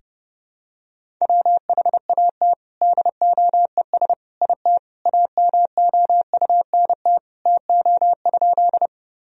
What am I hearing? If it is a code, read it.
Morse code, heard as a keyed tone, about 30 wpm, 700 Hz. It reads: WHAT DOES IT AMOUNT TO?
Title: Sentences from Top 1000 Words - Review Part 4 - 30wpm